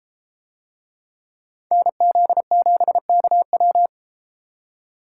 Morse code, heard as a keyed tone, about 32 wpm, 700 Hz. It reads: NZ7KW